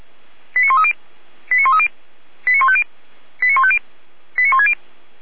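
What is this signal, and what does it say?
Five-tone selective call (ZVEI), heard as quick runs of five tones, tones of about 70 ms. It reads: 89130 89130 89150 89150 89160
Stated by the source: ZVEI five-tone selective call 89120, received by SDR.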